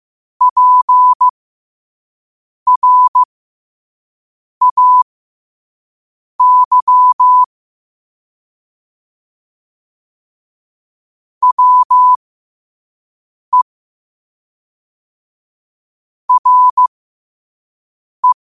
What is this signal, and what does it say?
Morse code, heard as a keyed tone, about 15 wpm, spaced as for 5 wpm, 1000 Hz. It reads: PRAY WE RE
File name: qotdfiles / 5wpm